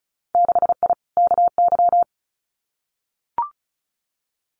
Morse code, heard as a keyed tone, about 34 wpm, 700 Hz. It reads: 6I KY
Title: Winter Field Day Exchanges 35wpm